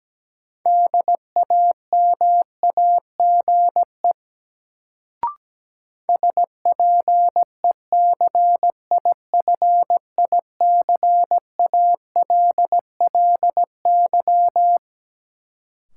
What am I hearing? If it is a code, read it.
Morse code, heard as a keyed tone, about 17 wpm, 700 Hz. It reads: DAMAGE SPECIFICALLY
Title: Sentences from Top 1100 Words - New Word Drill Alt - 17wpm